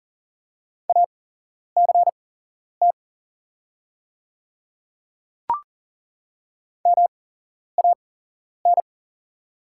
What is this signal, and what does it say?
Morse code, heard as a keyed tone, about 40 wpm, spaced as for 11 wpm, 700 Hz. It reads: ACT MAN
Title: 8x Character Spacing - Top 200 Words - 40wpm